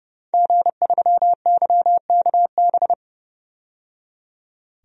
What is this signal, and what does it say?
Morse code, heard as a keyed tone, about 32 wpm, 700 Hz. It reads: G3YKB
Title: DXCC-100 Call Signs - 5 Characters - 30wpm